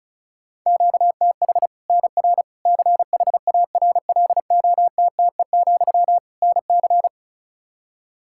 Morse code, heard as a keyed tone, about 36 wpm, 700 Hz. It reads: QTH NR CHARLOTTE, NC
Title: Practice QSO - Exchange 1 - Beginning - 35wpm